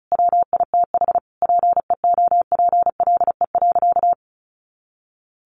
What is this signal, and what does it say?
Morse code, heard as a keyed tone, about 34 wpm, 700 Hz. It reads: WITH PEOPLE.
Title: Sentences from Top 1100 Words 35wpm